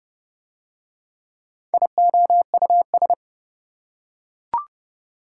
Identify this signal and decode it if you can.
Morse code, heard as a keyed tone, about 26 wpm, 700 Hz. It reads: IOUS